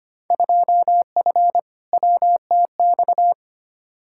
Morse code, heard as a keyed tone, about 25 wpm, 700 Hz. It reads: 2F WTX
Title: Field Day Exchanges - 25wpm